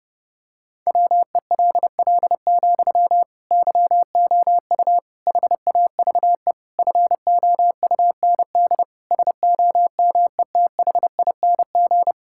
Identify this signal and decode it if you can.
Morse code, heard as a keyed tone, about 30 wpm, 700 Hz. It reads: WELL, YOU HAVE FOUND SOMETHING
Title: Sentences from Top 1000 Words - Review Part 6 - 30wpm